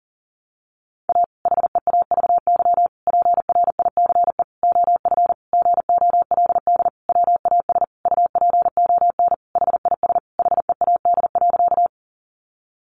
Morse code, heard as a keyed tone, about 37 wpm, 700 Hz. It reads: A HEAVY PRICE OF GOLD WAS UPON HIS HEAD.
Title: Sentences from Top 1100 Words 40wpm